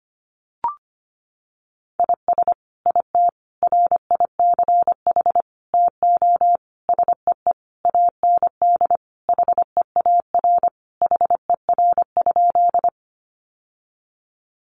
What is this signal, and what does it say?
Morse code, heard as a keyed tone, about 25 wpm, 700 Hz. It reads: IS IT RICH TO SEE AND HEAR HER?
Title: Sentences from Top 1100 Words 25wpm